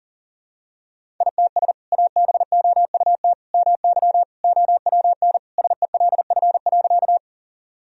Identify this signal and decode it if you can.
Morse code, heard as a keyed tone, about 40 wpm, 700 Hz. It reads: ITS ABOUT MY OWN SELF.
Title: Sentences from Top 1000 Words - Review Part 1 - 40wpm